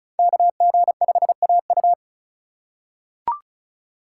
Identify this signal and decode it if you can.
Morse code, heard as a keyed tone, about 36 wpm, 700 Hz. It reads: KG5AU
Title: USA Call Signs - 5 Characters - 4x Character Spacing - 35wpm